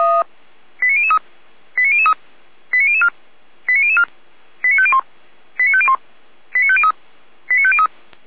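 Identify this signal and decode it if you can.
Five-tone selective call (ZVEI), heard as quick runs of five tones, tones of about 70 ms. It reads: 89003 89003 89004 89004 89501 89501 89503 89503
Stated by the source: ZVEI five-tone selective call received on SDR.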